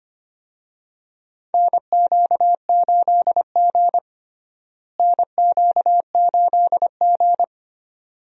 Morse code, heard as a keyed tone, about 25 wpm, 700 Hz. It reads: NQ8G NQ8G
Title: USA Call Signs - 4 Characters - Repeated 2x - 25wpm